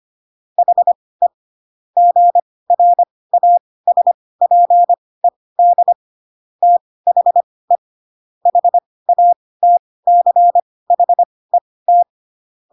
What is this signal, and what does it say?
Morse code, heard as a keyed tone, about 25 wpm, spaced as for 18 wpm, 700 Hz. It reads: HE GRASPED THE HATCHET
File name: Pinocchio / Pinocchio ch1 12 25 ik1hge